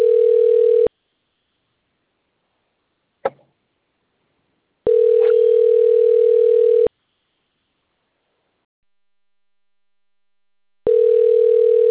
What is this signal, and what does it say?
Ringback tone (North America), 440 + 480 Hz.